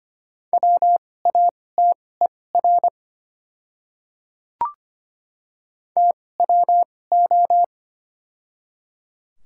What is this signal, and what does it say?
Morse code, heard as a keyed tone, about 25 wpm, spaced as for 18 wpm, 700 Hz. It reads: WATER TWO
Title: 2x Character Spacing - Top 100 Words - 25wpm